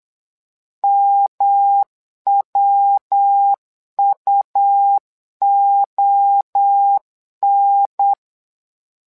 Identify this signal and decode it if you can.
Morse code, heard as a keyed tone, about 8 wpm, 800 Hz. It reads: MWUON
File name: LEZIONI / 3 MWRO / 3 MWRO 8wpm 2606